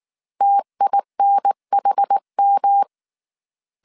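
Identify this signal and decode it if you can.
Morse code, heard as a keyed tone, about 19 wpm, 800 Hz. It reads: TINHM
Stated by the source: Morse code dictation at 18 wpm from the IW3SGT QRS Morse code training course, five-character groups of 300 characters.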